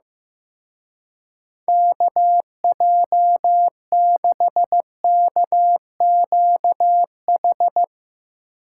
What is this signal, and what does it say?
Morse code, heard as a keyed tone, about 15 wpm, 700 Hz. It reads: KJ6KQH
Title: Speed Racing - USA 2x3 Call Signs 15wpm